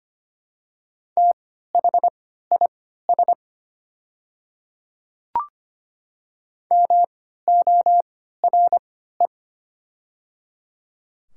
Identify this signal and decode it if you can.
Morse code, heard as a keyed tone, about 25 wpm, spaced as for 14 wpm, 700 Hz. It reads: THIS MORE
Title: 3x Character Spacing - Top 100 Words - 25wpm